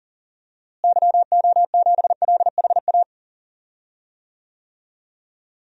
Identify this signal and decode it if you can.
Morse code, heard as a keyed tone, about 40 wpm, 700 Hz. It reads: YO7LHA
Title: DXCC-100 Call Signs - 6 Characters - Repeated 3x - 40wpm